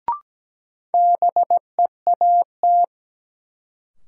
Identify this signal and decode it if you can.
Morse code, heard as a keyed tone, about 17 wpm, 700 Hz. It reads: BEAT
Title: Sentences from Top 1300 Words - New Word Drill - 17wpm